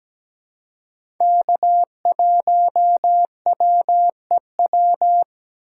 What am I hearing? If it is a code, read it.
Morse code, heard as a keyed tone, about 17 wpm, 700 Hz. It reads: K1WEW
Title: USA Call Signs - 4x Character Spacing - 17wpm